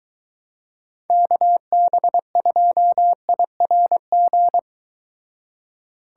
Morse code, heard as a keyed tone, about 22 wpm, 700 Hz. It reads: KB2IRG